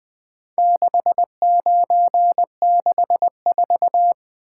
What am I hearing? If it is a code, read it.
Morse code, heard as a keyed tone, about 20 wpm, 700 Hz. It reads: 6964
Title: Four Letter-Number Combinations - 20wpm